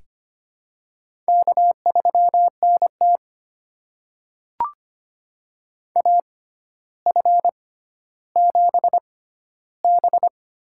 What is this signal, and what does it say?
Morse code, heard as a keyed tone, about 25 wpm, spaced as for 13 wpm, 700 Hz. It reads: K3NT AF7B
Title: USA Call Signs - 4 Characters - 6x Character Spacing - 25wpm